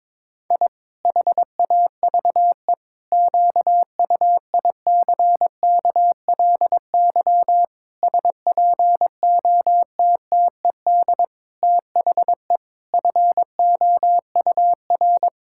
Morse code, heard as a keyed tone, about 21 wpm, 700 Hz. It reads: I HAVE QUICKLY SPOTTED THE FOUR
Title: Pangram Challenge v2 - 22wpm